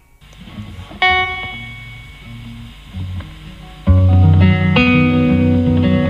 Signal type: dial-up modem handshake